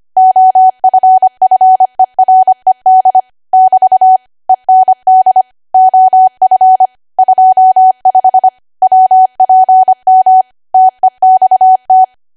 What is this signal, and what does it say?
Morse code, heard as a keyed tone, about 25 wpm, 750 Hz. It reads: OFFERED <BT> END OF 25 WPM TEXT
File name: Archive / 25 WPM / 25WPM